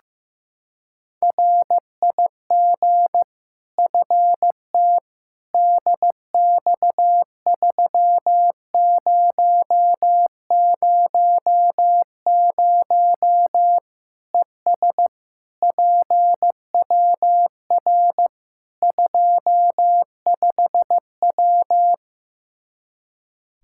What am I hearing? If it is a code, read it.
Morse code, heard as a keyed tone, about 15 wpm, 700 Hz. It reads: RIG FT DX3000 ES PWR 25W
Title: K1USN SST QSO Practice 15wpm